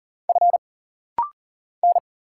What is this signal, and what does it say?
Morse code, heard as a keyed tone, about 40 wpm, 700 Hz. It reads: F N
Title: Single Letters - Rapid-Fire 40wpm